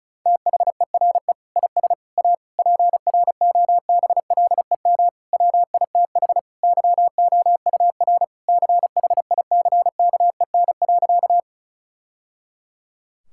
Morse code, heard as a keyed tone, about 36 wpm, 700 Hz. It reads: THERE IS A PROBLEM WITH YOUR CHICKEN.
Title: Sentences from Top 1000 Words - Review Part 5 - 35wpm